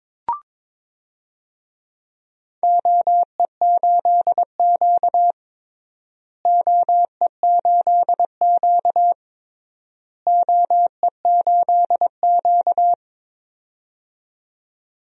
Morse code, heard as a keyed tone, about 22 wpm, 700 Hz. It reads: OE8Q OE8Q OE8Q
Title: DXCC-100 Call Signs - 4 Characters - Repeated 3x - 22wpm